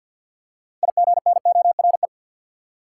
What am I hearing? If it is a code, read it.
Morse code, heard as a keyed tone, about 53 wpm, 700 Hz. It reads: IGNORE
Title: Six Letter Words - 50wpm